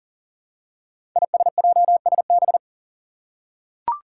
Morse code, heard as a keyed tone, about 40 wpm, 700 Hz. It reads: ISJSB
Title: Five Challenging Letter-Number Combinations 40wpm